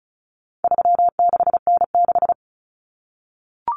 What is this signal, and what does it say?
Morse code, heard as a keyed tone, about 34 wpm, 700 Hz. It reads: 36N6
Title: Common Cut Numbers - 4-Digit 35wpm